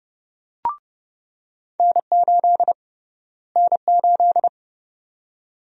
Morse code, heard as a keyed tone, about 31 wpm, 700 Hz. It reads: N8 N8
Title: USA Call Sign Prefixes - Repeated 2x - 30wpm